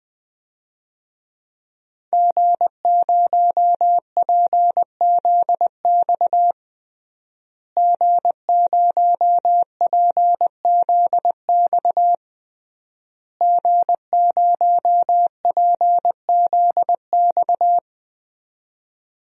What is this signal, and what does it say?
Morse code, heard as a keyed tone, about 20 wpm, 700 Hz. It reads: G0PZX G0PZX G0PZX
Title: DXCC-100 Call Signs - 5 Characters - Repeated 3x - 20wpm